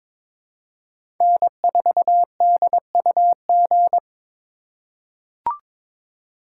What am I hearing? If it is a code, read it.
Morse code, heard as a keyed tone, about 22 wpm, 700 Hz. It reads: N4DUG
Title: USA Call Signs - 5 Characters - 2x Character Spacing - 22wpm